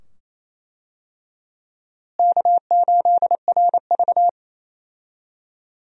Morse code, heard as a keyed tone, about 28 wpm, 700 Hz. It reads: K8RV
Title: USA Call Signs - 4 Characters - Repeated 2x - 28wpm